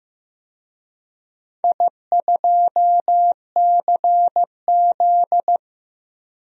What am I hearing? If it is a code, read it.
Morse code, heard as a keyed tone, about 15 wpm, 700 Hz. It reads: I2CZ
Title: DXCC-100 Call Signs - 4 Characters - Repeated 3x - 15wpm